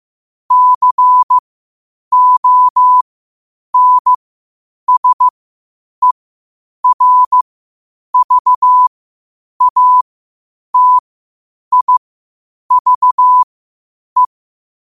Morse code, heard as a keyed tone, about 15 wpm, spaced as for 8 wpm, 1000 Hz. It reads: CONSERVATIVE
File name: qotdfiles / 8wpm